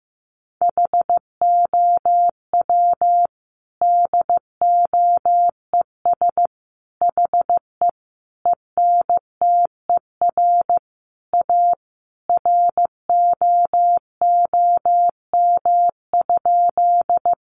Morse code, heard as a keyed tone, about 15 wpm, 700 Hz. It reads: HOW DOES HE ENTER A ROOM?
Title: Sentences from Top 1300 Words 15wpm